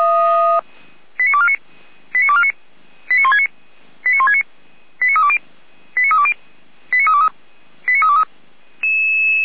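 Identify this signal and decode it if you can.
Five-tone selective call (ZVEI), heard as quick runs of five tones, tones of about 70 ms. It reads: 89249 89249 89169 89169 89320 89320 89323 89323
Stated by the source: ZVEI five-tone selective call received with an SDR.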